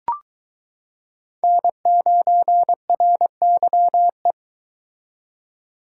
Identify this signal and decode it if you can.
Morse code, heard as a keyed tone, about 22 wpm, 700 Hz. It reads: N9RYE